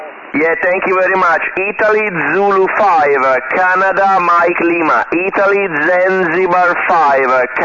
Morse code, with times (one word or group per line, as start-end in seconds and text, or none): none